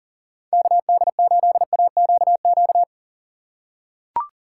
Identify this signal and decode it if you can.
Morse code, heard as a keyed tone, about 40 wpm, 700 Hz. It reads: KD8AQQ
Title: USA Call Signs - 6 Characters - 2x Character Spacing - 40wpm